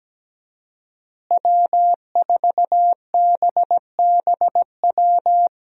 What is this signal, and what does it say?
Morse code, heard as a keyed tone, about 17 wpm, 700 Hz. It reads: W4BBW
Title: USA Call Signs - 5 Characters - Repeated 3x - 17wpm